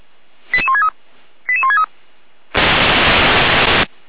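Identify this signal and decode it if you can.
Five-tone selective call (ZVEI), heard as quick runs of five tones, tones of about 70 ms. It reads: 88163 88163
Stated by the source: ZVEI five-tone call 88207, received by SDR, on the feuerwehr-orlamuende fire brigade server.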